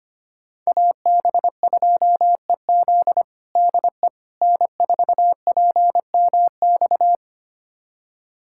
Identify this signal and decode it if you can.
Morse code, heard as a keyed tone, about 25 wpm, 700 Hz. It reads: AB2EZ DE N4PMX
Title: USA Callsign DE Callsign - 5 Characters - 25wpm